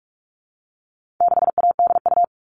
Morse code, heard as a keyed tone, about 44 wpm, 700 Hz. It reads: 6ADU
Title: All Cut Numbers - 4-Digit 45wpm